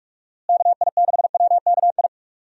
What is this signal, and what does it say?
Morse code, heard as a keyed tone, about 46 wpm, 700 Hz. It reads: KI6WKI
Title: USA Call Signs - 6 Characters - Repeated 3x - 45wpm